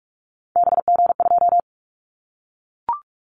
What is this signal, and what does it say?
Morse code, heard as a keyed tone, about 44 wpm, 700 Hz. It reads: BG2
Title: Callsign Prefixes for Top 100 DXCC - 45wpm